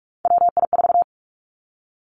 Morse code, heard as a keyed tone, about 46 wpm, 700 Hz. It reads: WI4
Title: Three Challenging Letter-Number Combinations 45wpm